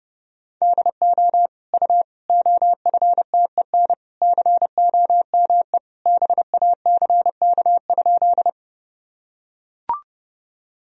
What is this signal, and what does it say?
Morse code, heard as a keyed tone, about 30 wpm, 700 Hz. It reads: DO U OFTEN COME BACK?
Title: Sentences from Top 300 Words and Abbreviations 30wpm